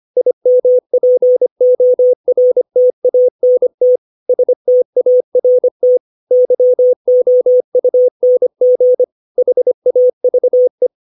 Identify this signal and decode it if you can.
Morse code, heard as a keyed tone, about 25 wpm, 500 Hz. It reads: IMPORTANT START YOUNG HAVE